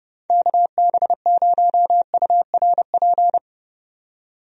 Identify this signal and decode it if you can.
Morse code, heard as a keyed tone, about 30 wpm, 700 Hz. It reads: KB0URP
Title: USA Call Signs - 6 Characters - 30wpm